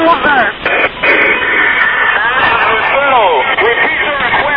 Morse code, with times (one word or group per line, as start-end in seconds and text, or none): none